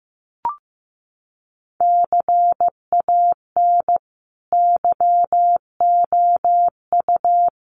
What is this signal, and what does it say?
Morse code, heard as a keyed tone, about 15 wpm, 700 Hz. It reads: CAN YOU
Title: Sentences from Top 1400 Words 15wpm